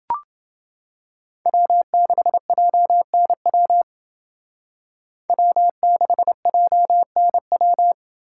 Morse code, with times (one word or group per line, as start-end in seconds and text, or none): 1.5-3.8 W6JNW
5.3-7.9 W6JNW